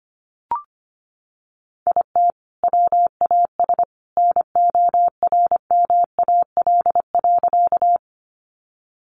Morse code, heard as a keyed tone, about 25 wpm, 700 Hz. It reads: IT WAS NORMAL.
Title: Sentences from Top 1200 Words 25wpm